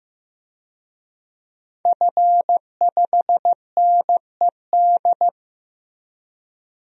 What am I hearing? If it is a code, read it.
Morse code, heard as a keyed tone, about 15 wpm, 700 Hz. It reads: F5NED
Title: DXCC-100 Call Signs - 5 Characters - Repeated 3x - 15wpm